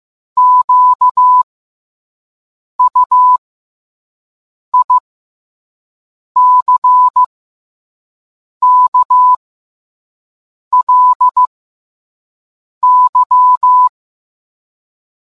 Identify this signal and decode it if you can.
Morse code, heard as a keyed tone, about 15 wpm, spaced as for 5 wpm, 1000 Hz. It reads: QUICKLY